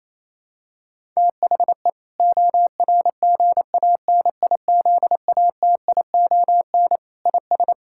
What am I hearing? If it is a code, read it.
Morse code, heard as a keyed tone, about 28 wpm, 700 Hz. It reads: THE ORGANIZATION IS
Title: Sentences from Top 1000 Words - Review Part 5 - 28wpm